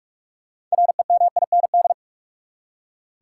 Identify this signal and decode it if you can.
Morse code, heard as a keyed tone, about 46 wpm, 700 Hz. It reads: REMIND
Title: Six Letter Words - 45wpm